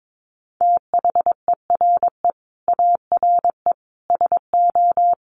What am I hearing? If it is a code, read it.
Morse code, heard as a keyed tone, about 22 wpm, 700 Hz. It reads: THERE ARE SO
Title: Additional Easy Phrases 22wpm